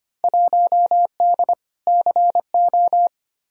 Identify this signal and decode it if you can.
Morse code, heard as a keyed tone, about 25 wpm, 700 Hz. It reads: 1D CO
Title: Field Day Exchanges - 25wpm